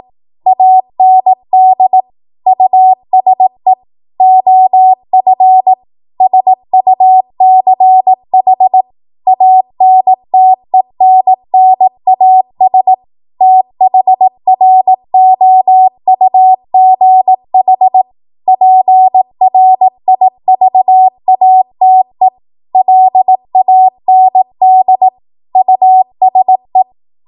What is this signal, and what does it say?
Morse code, heard as a keyed tone, about 18 wpm, 750 Hz. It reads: AND USE OF SUCH ANTENNAS THROUGH PRIVATE LAND USE
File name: Morse / 18WPM